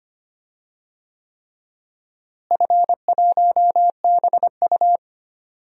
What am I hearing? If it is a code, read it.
Morse code, heard as a keyed tone, about 25 wpm, 700 Hz. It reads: F1BU
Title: DXCC-100 Call Signs - 4 Characters - Repeated 3x - 25wpm